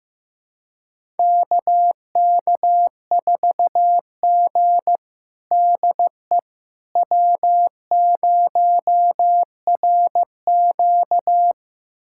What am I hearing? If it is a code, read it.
Morse code, heard as a keyed tone, about 15 wpm, 700 Hz. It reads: KK4G DE W0RQ
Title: USA Callsign DE Callsign - 4 Characters - 15wpm